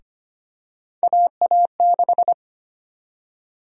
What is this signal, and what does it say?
Morse code, heard as a keyed tone, about 25 wpm, 700 Hz. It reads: AA6
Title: USA Call Sign Prefixes - Repeated 2x - 25wpm